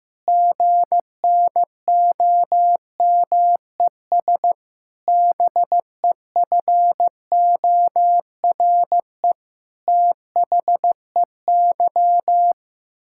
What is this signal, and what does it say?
Morse code, heard as a keyed tone, about 15 wpm, 700 Hz. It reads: GNOMES BEFORE THEY
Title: Pangram Challenge v2 - 15wpm